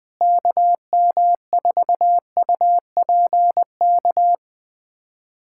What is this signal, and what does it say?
Morse code, heard as a keyed tone, about 20 wpm, 700 Hz. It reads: KM4UPK
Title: USA Callsign DE Callsign - 6 Characters - 20wpm